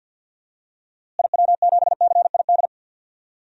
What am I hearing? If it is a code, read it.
Morse code, heard as a keyed tone, about 53 wpm, 700 Hz. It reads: IW7CID
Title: DXCC-100 Call Signs - 6 Characters - Repeated 2x - 50wpm